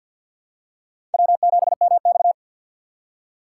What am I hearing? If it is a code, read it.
Morse code, heard as a keyed tone, about 53 wpm, 700 Hz. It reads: W7MX
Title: USA Call Signs - 4 Characters - 6x Character Spacing - 50wpm